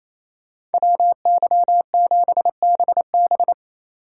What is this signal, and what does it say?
Morse code, heard as a keyed tone, about 35 wpm, 700 Hz. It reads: WY7BB